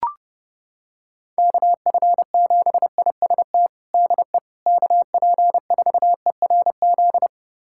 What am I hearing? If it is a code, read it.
Morse code, heard as a keyed tone, about 28 wpm, 700 Hz. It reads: KF7IST DE KP4ERZ
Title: USA Callsign DE Callsign - 6 Characters - 30wpm